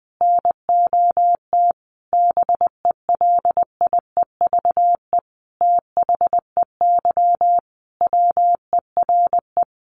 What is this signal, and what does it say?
Morse code, heard as a keyed tone, about 20 wpm, 700 Hz. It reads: NOT BELIEVE THEY WERE
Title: Sentences from Top 1000 Words 20wpm